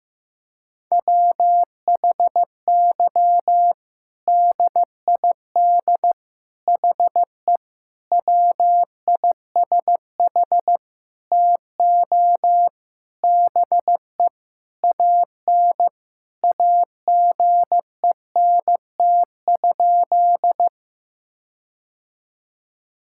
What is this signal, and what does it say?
Morse code, heard as a keyed tone, about 15 wpm, 700 Hz. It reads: WHY DID HE WISH TO BE AN AGENT?
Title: Sentences from Top 1600 Words 15wpm